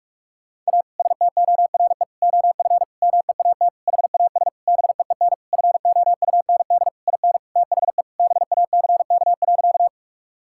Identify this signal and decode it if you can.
Morse code, heard as a keyed tone, about 47 wpm, 700 Hz. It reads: A STORE OF MEAT HAS BEEN FOUND IN THE BACK.
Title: Sentences from Top 1000 Words - Review Part 5 - 45wpm